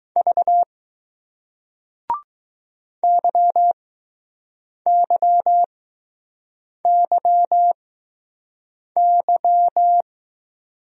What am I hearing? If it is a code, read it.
Morse code, heard as a keyed tone, about 20 wpm, spaced as for 6 wpm, 700 Hz. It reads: V YYYY